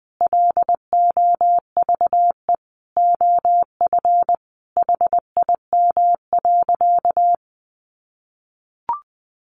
Morse code, heard as a keyed tone, about 20 wpm, 700 Hz. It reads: LOVE OF HIM.
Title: Sentences from Top 1500 Words 20wpm